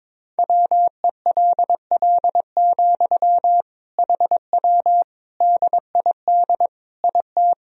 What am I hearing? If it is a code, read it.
Morse code, heard as a keyed tone, about 22 wpm, 700 Hz. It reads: WELL, HW DID IT
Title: Sentences from Top 200 Words and Abbreviations 22wpm